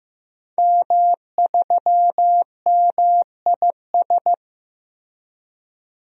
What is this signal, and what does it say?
Morse code, heard as a keyed tone, about 15 wpm, 700 Hz. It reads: M3MIS
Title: DXCC-100 Call Signs - 5 Characters - Repeated 2x - 15wpm